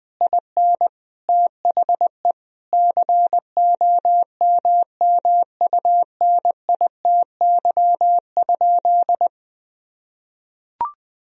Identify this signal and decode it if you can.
Morse code, heard as a keyed tone, about 20 wpm, 700 Hz. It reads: IN THE COMMUNITY?
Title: Sentences from Top 1000 Words - Review Part 4 - 20wpm